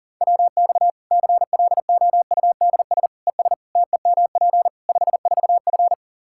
Morse code, heard as a keyed tone, about 37 wpm, 700 Hz. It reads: WX CLOUDS ES TEMP 54F
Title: Practice QSO - Exchange 2 - Info - 40wpm